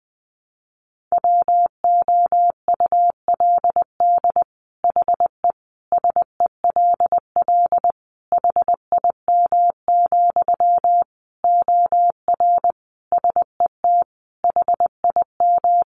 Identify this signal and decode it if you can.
Morse code, heard as a keyed tone, about 19 wpm, 700 Hz. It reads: WOULD HE SELL HIM, OR SET HIM
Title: Sentences from Top 1100 Words 20wpm